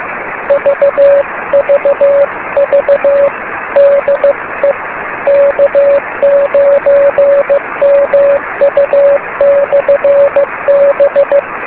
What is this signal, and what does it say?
Morse code, heard as a keyed tone, about 15 wpm, 560 Hz, with noise. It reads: VVV DE K9MU/B
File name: beacons / k9mu